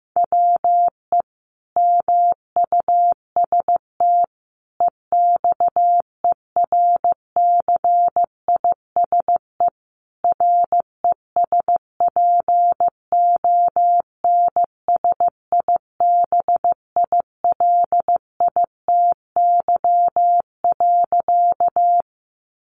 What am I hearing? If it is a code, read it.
Morse code, heard as a keyed tone, about 15 wpm, 700 Hz. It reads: WE MUST EXERCISE RESPONSIBILITY.
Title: Sentences from Top 1200 Words 15wpm